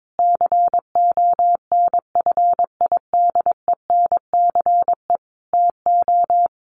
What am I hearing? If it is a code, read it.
Morse code, heard as a keyed tone, about 22 wpm, 700 Hz. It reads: CONFIDENCE TO
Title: Sentences from Top 1500 Words 22wpm